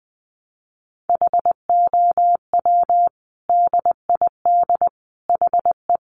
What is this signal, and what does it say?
Morse code, heard as a keyed tone, about 19 wpm, 700 Hz. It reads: HOW DID HE
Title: Sentences from Top 1100 Words 20wpm